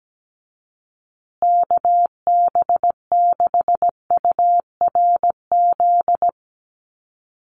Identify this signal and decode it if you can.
Morse code, heard as a keyed tone, about 17 wpm, 700 Hz. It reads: KB6URZ
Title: USA Call Signs Repeated 3x 17wpm